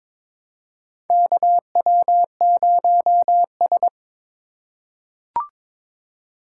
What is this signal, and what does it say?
Morse code, heard as a keyed tone, about 22 wpm, 700 Hz. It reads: KW0S
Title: USA Call Signs - 4 Characters - 2x Character Spacing - 22wpm